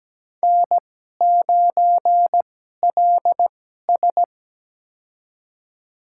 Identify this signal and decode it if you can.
Morse code, heard as a keyed tone, about 17 wpm, spaced as for 12 wpm, 700 Hz. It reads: N9LS
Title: USA Call Signs - 2x Character Spacing - 17wpm